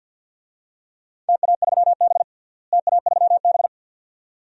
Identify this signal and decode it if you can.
Morse code, heard as a keyed tone, about 51 wpm, 700 Hz. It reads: TA3B TA3B
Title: DXCC-100 Call Signs - 4 Characters - Repeated 2x - 50wpm